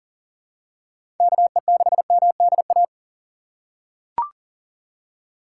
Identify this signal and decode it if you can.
Morse code, heard as a keyed tone, about 40 wpm, 700 Hz. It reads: KE6MDA
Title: USA Call Signs - 6 Characters - 6x Character Spacing - 40wpm